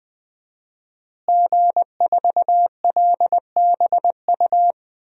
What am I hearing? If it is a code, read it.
Morse code, heard as a keyed tone, about 20 wpm, 700 Hz. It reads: G4LBU
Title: DXCC-100 Call Signs - 5 Characters - Repeated 3x - 20wpm